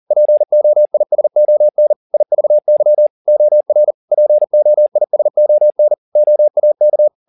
Morse code, heard as a keyed tone, about 40 wpm, 600 Hz. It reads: POISON IVY OR POISON OAK